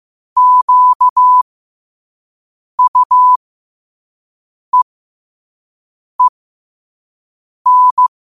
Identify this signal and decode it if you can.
Morse code, heard as a keyed tone, about 15 wpm, spaced as for 5 wpm, 1000 Hz. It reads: QUEEN
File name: qotdfiles / 5wpm